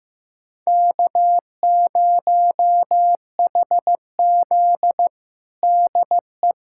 Morse code, heard as a keyed tone, about 15 wpm, 700 Hz. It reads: K0HZ DE
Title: USA Callsign DE Callsign - 4 Characters - 15wpm